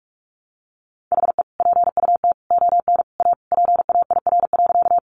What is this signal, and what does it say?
Morse code, heard as a keyed tone, about 44 wpm, 700 Hz. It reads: HE PUT ON A PAIR.
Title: Sentences from Top 1200 Words 45wpm